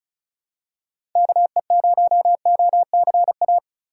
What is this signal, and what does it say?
Morse code, heard as a keyed tone, about 34 wpm, 700 Hz. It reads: KE0OCA
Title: USA Call Signs - 6 Characters - 3x Character Spacing - 35wpm